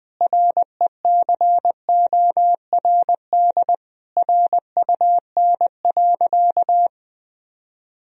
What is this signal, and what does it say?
Morse code, heard as a keyed tone, about 20 wpm, 700 Hz. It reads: RECORD RUN.
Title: Sentences from Top 1000 Words - Review Part 3 - 20wpm